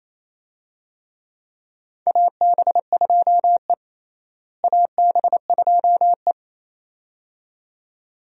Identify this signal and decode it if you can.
Morse code, heard as a keyed tone, about 28 wpm, 700 Hz. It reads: AB2E AB2E
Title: USA Call Signs - 4 Characters - Repeated 2x - 28wpm